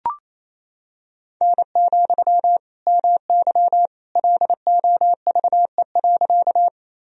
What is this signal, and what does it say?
Morse code, heard as a keyed tone, about 28 wpm, 700 Hz. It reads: N, MY LOVE.
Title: Sentences from Top 400 Words and Abbreviations 28wpm